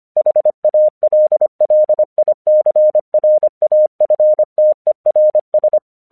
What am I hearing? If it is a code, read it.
Morse code, heard as a keyed tone, about 25 wpm, 610 Hz. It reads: HALLICRAFTERS
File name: 1 English 25wpm / AQC12